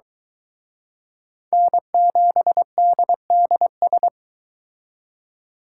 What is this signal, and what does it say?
Morse code, heard as a keyed tone, about 24 wpm, 700 Hz. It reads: N7DDS